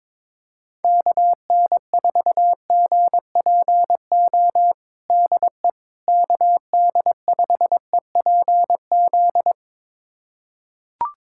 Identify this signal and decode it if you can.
Morse code, heard as a keyed tone, about 22 wpm, 700 Hz. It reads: KN4GPO DE KD5EPZ
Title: USA Callsign DE Callsign - 6 Characters - 22wpm